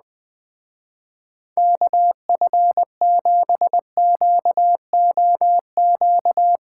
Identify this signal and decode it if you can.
Morse code, heard as a keyed tone, about 19 wpm, 700 Hz. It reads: KF7QOQ
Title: Speed Racing - USA 2x3 Call Signs 17wpm